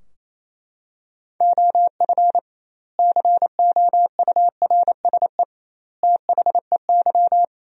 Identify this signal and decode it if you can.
Morse code, heard as a keyed tone, about 28 wpm, 700 Hz. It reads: OF COURSE THEY
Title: 2x Word-Spacing - Sentences from Top 500 Words - 28wpm